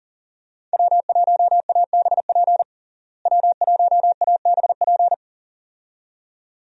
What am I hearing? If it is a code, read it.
Morse code, heard as a keyed tone, about 40 wpm, 700 Hz. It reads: W1ABP W1ABP
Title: USA Call Signs - 5 Characters - Repeated 2x - 40wpm